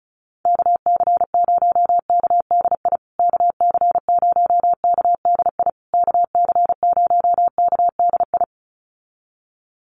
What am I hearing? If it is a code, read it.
Morse code, heard as a keyed tone, about 34 wpm, 700 Hz. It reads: KC0KDI KC0KDI KC0KDI
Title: USA Call Signs Repeated 3x 35wpm